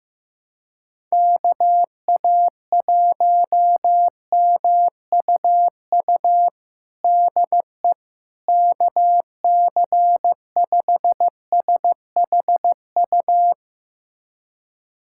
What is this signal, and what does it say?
Morse code, heard as a keyed tone, about 15 wpm, 700 Hz. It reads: KA1MUU DE KC5SHU
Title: USA Callsign DE Callsign - 6 Characters - 15wpm